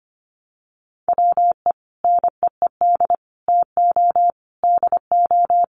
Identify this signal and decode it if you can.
Morse code, heard as a keyed tone, about 25 wpm, 700 Hz. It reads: WE NEED TO DO
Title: Sets of 5 Words Encore2 25wpm